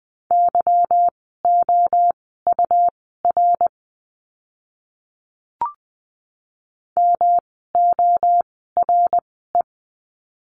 Morse code, heard as a keyed tone, about 20 wpm, spaced as for 15 wpm, 700 Hz. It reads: YOUR MORE
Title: 206 Lesson 33f - Letter j - Words 20wpm